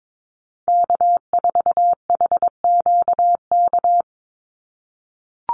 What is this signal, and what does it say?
Morse code, heard as a keyed tone, about 22 wpm, 700 Hz. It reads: K4HQK
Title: Call Signs - 22wpm